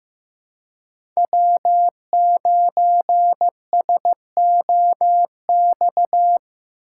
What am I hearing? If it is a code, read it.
Morse code, heard as a keyed tone, about 18 wpm, 700 Hz. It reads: W9SOX
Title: Speed Racing - USA 1x3 Call Signs 15wpm